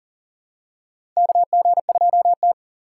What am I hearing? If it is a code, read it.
Morse code, heard as a keyed tone, about 40 wpm, 700 Hz. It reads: KG2T
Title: USA Call Signs - 4 Characters - 3x Character Spacing - 40wpm